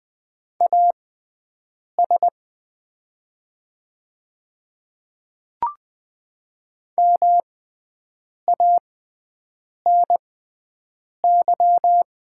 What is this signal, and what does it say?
Morse code, heard as a keyed tone, about 20 wpm, spaced as for 7 wpm, 700 Hz. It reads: AS MANY
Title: 6x Character Spacing - Top 100 Words - 20wpm